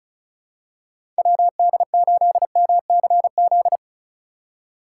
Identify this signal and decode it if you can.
Morse code, heard as a keyed tone, about 36 wpm, 700 Hz. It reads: WD8MCZ
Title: USA Call Signs - 6 Characters - 2x Character Spacing - 35wpm